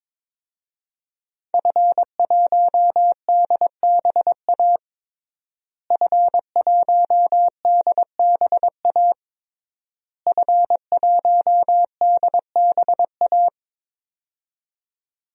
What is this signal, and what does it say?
Morse code, heard as a keyed tone, about 22 wpm, 700 Hz. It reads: F1DBA F1DBA F1DBA
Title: DXCC-100 Call Signs - 5 Characters - Repeated 3x - 22wpm